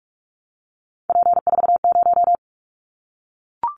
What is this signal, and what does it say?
Morse code, heard as a keyed tone, about 44 wpm, 700 Hz. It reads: P40
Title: Callsign Prefixes for Top 100 DXCC - 45wpm